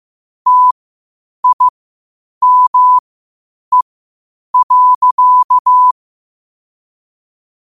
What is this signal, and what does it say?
Morse code, heard as a keyed tone, about 15 wpm, spaced as for 8 wpm, 1000 Hz. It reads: TIME.
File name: qotdfiles / 8wpm